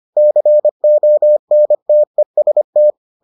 Morse code, heard as a keyed tone, about 25 wpm, 600 Hz. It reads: CONTEST